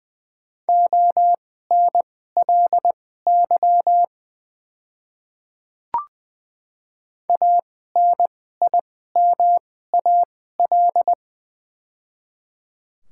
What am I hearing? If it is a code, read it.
Morse code, heard as a keyed tone, about 20 wpm, spaced as for 15 wpm, 700 Hz. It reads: ONLY ANIMAL
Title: 2x Character Spacing - Top 200 Words - 20wpm